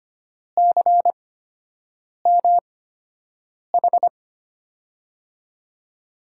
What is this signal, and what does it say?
Morse code, heard as a keyed tone, about 25 wpm, spaced as for 7 wpm, 700 Hz. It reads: CMH
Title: USA Call Signs - 5 Characters - 8x Character Spacing - 25wpm